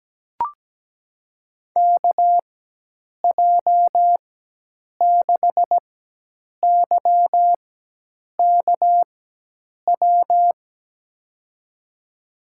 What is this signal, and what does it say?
Morse code, heard as a keyed tone, about 17 wpm, spaced as for 8 wpm, 700 Hz. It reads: KJ6YKW
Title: USA Call Signs - 6 Characters - 4x Character Spacing - 17wpm